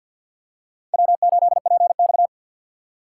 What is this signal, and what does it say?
Morse code, heard as a keyed tone, about 53 wpm, 700 Hz. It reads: W8PX
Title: USA Call Signs - 4 Characters - 2x Character Spacing - 50wpm